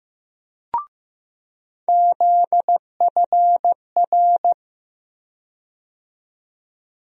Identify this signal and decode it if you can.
Morse code, heard as a keyed tone, about 15 wpm, 700 Hz. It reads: ZFR
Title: USA Call Sign Postfixes - 3 Letters 15wpm